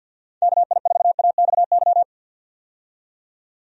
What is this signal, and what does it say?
Morse code, heard as a keyed tone, about 46 wpm, 700 Hz. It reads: KI4AXY
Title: USA Call Signs - 6 Characters - 50wpm